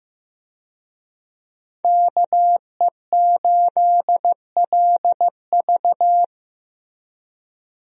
Morse code, heard as a keyed tone, about 15 wpm, 700 Hz. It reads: KE8LV
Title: USA Call Signs - 5 Characters - 15wpm